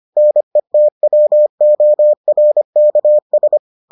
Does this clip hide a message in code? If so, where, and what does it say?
Yes: Morse code from 0.2 to 3.6 s, heard as a keyed tone, about 25 wpm, 600 Hz. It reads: NETWORKS